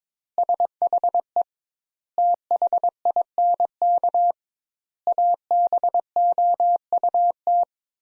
Morse code, heard as a keyed tone, about 22 wpm, 700 Hz. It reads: SHE THINK ABOUT
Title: Sentences Top 200 Words - 22wpm